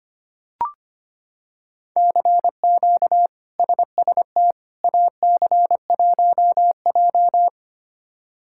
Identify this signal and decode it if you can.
Morse code, heard as a keyed tone, about 25 wpm, 700 Hz. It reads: CQ SST AC1J
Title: K1USN SST QSO Practice - 25wpm